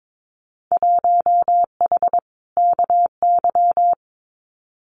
Morse code, heard as a keyed tone, about 22 wpm, 700 Hz. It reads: 1H KY
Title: Winter Field Day Exchanges 22wpm